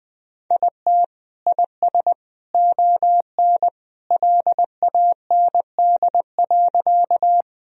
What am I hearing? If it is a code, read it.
Morse code, heard as a keyed tone, about 21 wpm, 700 Hz. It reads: IT IS ON LAND.
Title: Sentences from Top 1000 Words - Review Part 1 - 20wpm